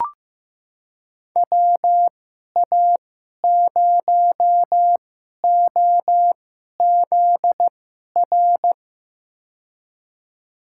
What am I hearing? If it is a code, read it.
Morse code, heard as a keyed tone, about 15 wpm, spaced as for 11 wpm, 700 Hz. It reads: WA0OZR
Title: USA Call Signs - 6 Characters - 2x Character Spacing - 15wpm